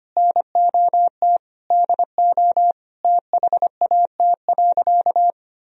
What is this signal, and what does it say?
Morse code, heard as a keyed tone, about 25 wpm, 700 Hz. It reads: NOT DO THAT.
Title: Sentences from Top 1000 Words - Review Part 4 - 25wpm